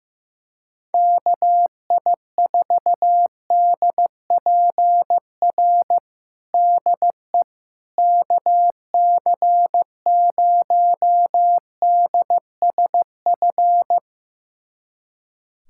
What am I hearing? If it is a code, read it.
Morse code, heard as a keyed tone, about 15 wpm, 700 Hz. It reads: KI4DPR DE KC0DSF
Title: USA Callsign DE Callsign 15wpm